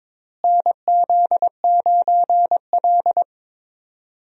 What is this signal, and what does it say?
Morse code, heard as a keyed tone, about 22 wpm, 700 Hz. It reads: NZ9L
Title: USA Call Signs - 4 Characters - Repeated 3x - 22wpm